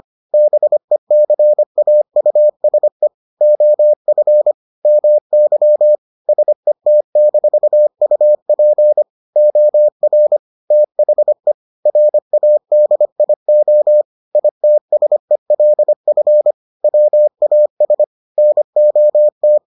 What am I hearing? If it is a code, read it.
Morse code, heard as a keyed tone, about 25 wpm, 600 Hz. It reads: BECAUSE OF MY SET-UP OR THE RADIO ITSELF WAS NOT